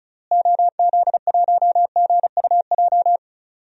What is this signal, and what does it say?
Morse code, heard as a keyed tone, about 36 wpm, 700 Hz. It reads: OZ1GUJ
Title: DXCC-100 Call Signs - 6 Characters - Repeated 3x - 35wpm